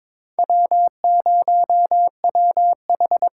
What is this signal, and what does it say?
Morse code, heard as a keyed tone, about 22 wpm, 700 Hz. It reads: W0WH
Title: USA Call Signs - 5 Characters - 8x Character Spacing - 22wpm